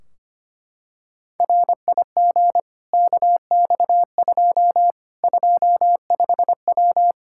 Morse code, heard as a keyed tone, about 25 wpm, 700 Hz. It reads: RIG KX2 25W
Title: Practice QSO - Exchange 2 - Info - 25wpm